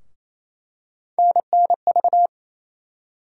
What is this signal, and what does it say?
Morse code, heard as a keyed tone, about 28 wpm, 700 Hz. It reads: NNV
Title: USA Call Sign Postfixes - 3 Letters 28wpm